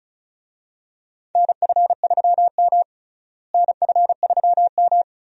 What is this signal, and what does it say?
Morse code, heard as a keyed tone, about 36 wpm, 700 Hz. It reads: NF3M NF3M
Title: USA Call Signs - 4 Characters - Repeated 2x - 35wpm